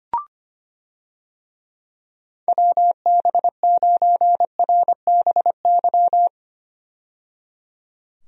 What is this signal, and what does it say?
Morse code, heard as a keyed tone, about 25 wpm, 700 Hz. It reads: WB9RBY
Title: USA Call Signs - 6 Characters - 25wpm